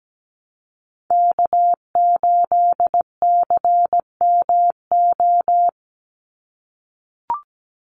Morse code, heard as a keyed tone, about 17 wpm, 700 Hz. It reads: K8CMO
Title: Call Signs - 17wpm